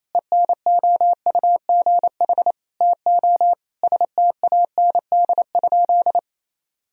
Morse code, heard as a keyed tone, about 28 wpm, 700 Hz. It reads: ENOUGH TO STAND?
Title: Sentences from Top 1000 Words - Review Part 3 - 28wpm